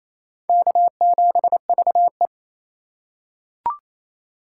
Morse code, heard as a keyed tone, about 28 wpm, 700 Hz. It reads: K7VE
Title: USA Call Signs - 4 Characters - 3x Character Spacing - 28wpm